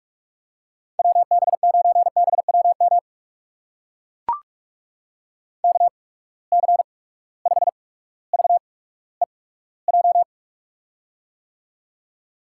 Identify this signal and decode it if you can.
Morse code, heard as a keyed tone, about 44 wpm, spaced as for 19 wpm, 700 Hz. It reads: WB9BWM KC5VEJ
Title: USA Call Signs - 6 Characters - 8x Character Spacing - 45wpm